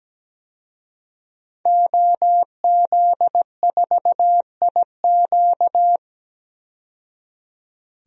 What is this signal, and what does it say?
Morse code, heard as a keyed tone, about 17 wpm, 700 Hz. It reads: OZ4IQ
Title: DXCC-100 Call Signs - 5 Characters - 17wpm